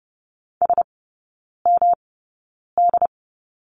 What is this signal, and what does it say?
Morse code, heard as a keyed tone, about 30 wpm, spaced as for 9 wpm, 700 Hz. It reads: SMD
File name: Lesson 19e - Letter w - Three Characters 30wpm